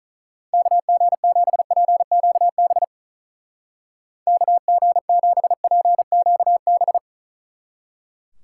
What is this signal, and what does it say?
Morse code, heard as a keyed tone, about 45 wpm, 700 Hz. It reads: KG7PQB KG7PQB